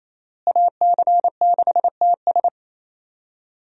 Morse code, heard as a keyed tone, about 27 wpm, 700 Hz. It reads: AC6TS
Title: USA Call Signs - 5 Characters - Repeated 2x - 28wpm